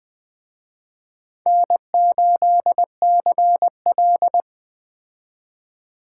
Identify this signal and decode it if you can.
Morse code, heard as a keyed tone, about 20 wpm, 700 Hz. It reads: N8CL